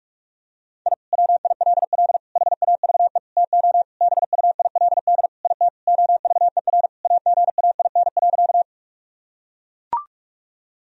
Morse code, heard as a keyed tone, about 41 wpm, 700 Hz. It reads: I WILL HAVE TO BUILD IT OVER AGAIN.
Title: Sentences from Top 1000 Words - Review Part 1 - 45wpm